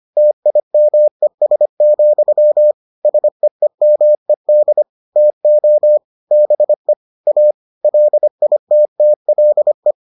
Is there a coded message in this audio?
Yes: Morse code from 0.2 to 9.9 s, heard as a keyed tone, about 25 wpm, 600 Hz. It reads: TIMES, SEEMED TO BE A LITTLE